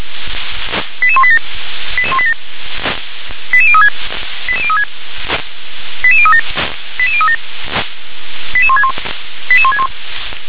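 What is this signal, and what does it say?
Five-tone selective call (ZVEI), heard as quick runs of five tones, tones of about 70 ms. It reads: 88187 88036 88036 88037 88037 88161 88161
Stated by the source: ZVEI five-tone selective call received by SDR.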